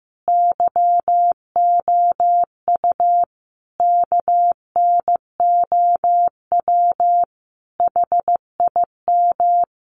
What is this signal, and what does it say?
Morse code, heard as a keyed tone, about 15 wpm, 700 Hz. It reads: YOU KNOW HIM
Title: Sentences from Top 1000 Words 15wpm